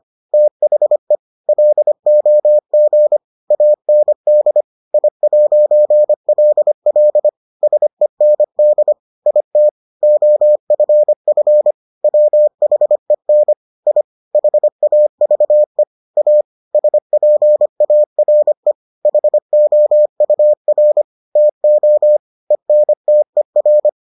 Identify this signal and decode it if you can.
Morse code, heard as a keyed tone, about 25 wpm, 600 Hz. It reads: THE LOG AND I'LL SEND IT OFF WHEN I HAVE A SPARE HOUR TO ENTER